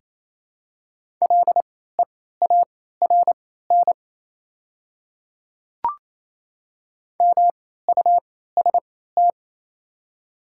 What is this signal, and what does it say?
Morse code, heard as a keyed tone, about 27 wpm, spaced as for 16 wpm, 700 Hz. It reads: LEARN MUST
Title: 3x Character Spacing - Top 300 Words - 28wpm